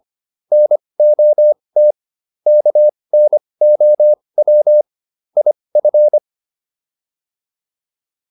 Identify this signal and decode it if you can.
Morse code, heard as a keyed tone, about 25 wpm, spaced as for 20 wpm, 600 Hz. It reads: NOT KNOW IF